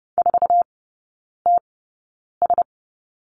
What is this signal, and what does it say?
Morse code, heard as a keyed tone, about 30 wpm, spaced as for 9 wpm, 690 Hz. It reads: 4TS
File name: Lesson 17e - Letter u - Three Characters 30wpm